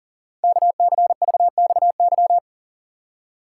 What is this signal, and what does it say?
Morse code, heard as a keyed tone, about 40 wpm, 690 Hz. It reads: KCVXY
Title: Code Groups - 5 Letters - 40wpm